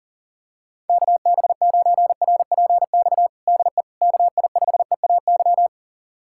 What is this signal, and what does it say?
Morse code, heard as a keyed tone, about 40 wpm, 700 Hz. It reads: KB9RPX DE KI5EAY
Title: USA Callsign DE Callsign - 6 Characters - 40wpm